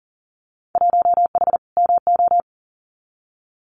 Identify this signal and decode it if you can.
Morse code, heard as a keyed tone, about 40 wpm, 700 Hz. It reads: 1H MO
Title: Winter Field Day Exchanges 40wpm